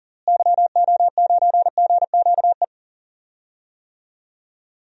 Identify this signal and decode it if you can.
Morse code, heard as a keyed tone, about 40 wpm, 700 Hz. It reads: YO9GQE
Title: DXCC-100 Call Signs - 6 Characters - Repeated 2x - 40wpm